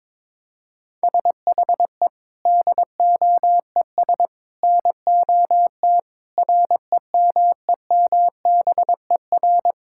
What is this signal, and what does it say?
Morse code, heard as a keyed tone, about 21 wpm, 700 Hz. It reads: SHE DOES NOT REMEMBER
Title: Sentences from Top 1000 Words - Review Part 5 - 22wpm